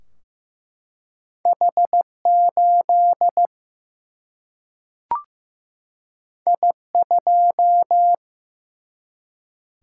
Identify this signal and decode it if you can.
Morse code, heard as a keyed tone, about 15 wpm, 700 Hz. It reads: H8 I2
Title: Single-Letter Single-Number 15wpm